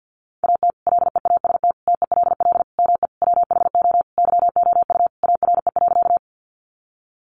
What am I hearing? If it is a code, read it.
Morse code, heard as a keyed tone, about 50 wpm, 700 Hz. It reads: AT LEAST TELL ME WHO YOU ARE.
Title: Sentences from Top 700 Words 50wpm